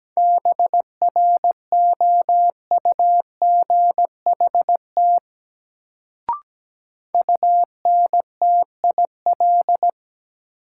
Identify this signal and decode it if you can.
Morse code, heard as a keyed tone, about 17 wpm, 700 Hz. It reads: BROUGHT UNTIL
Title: Sentences from Top 500 Words - New Word Drill - 17wpm